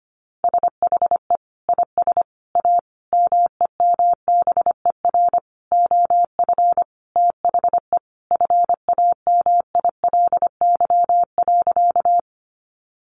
Morse code, heard as a keyed tone, about 25 wpm, 700 Hz. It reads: SHE IS A MEMBER OF THE FAMILY.
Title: Sentences from Top 1100 Words 25wpm